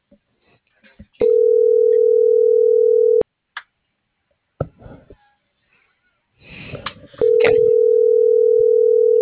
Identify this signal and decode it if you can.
Ringback tone (North America), 440 + 480 Hz.